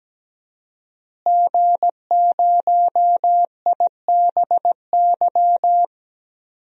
Morse code, heard as a keyed tone, about 17 wpm, 700 Hz. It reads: G0IBY
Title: DXCC-100 Call Signs - 5 Characters - Repeated 2x - 17wpm